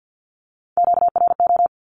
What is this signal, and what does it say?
Morse code, heard as a keyed tone, about 50 wpm, 700 Hz. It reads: QRO
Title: Q-Codes Repeated 3x 50wpm